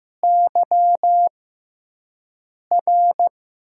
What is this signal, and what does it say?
Morse code, heard as a keyed tone, about 15 wpm, spaced as for 5 wpm, 700 Hz. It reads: YR
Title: USA Call Signs - 6 Characters - 6x Character Spacing - 15wpm